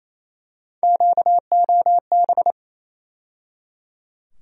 Morse code, heard as a keyed tone, about 28 wpm, 700 Hz. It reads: QOB
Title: USA Call Sign Postfixes - 3 Letters 28wpm